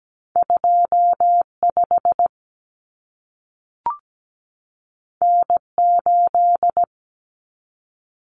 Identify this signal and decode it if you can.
Morse code, heard as a keyed tone, about 17 wpm, 700 Hz. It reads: 25 N8
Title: Common Cut Numbers - 2-Digit 17wpm